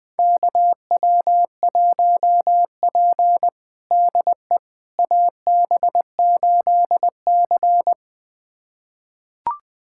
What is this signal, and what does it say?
Morse code, heard as a keyed tone, about 20 wpm, 700 Hz. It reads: KW1P DE AB8C
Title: USA Callsign DE Callsign - 4 Characters - 20wpm